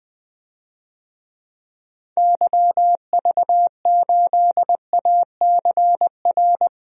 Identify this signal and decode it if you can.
Morse code, heard as a keyed tone, about 20 wpm, 700 Hz. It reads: YV8ACR
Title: DXCC-100 Call Signs - 6 Characters - Repeated 3x - 20wpm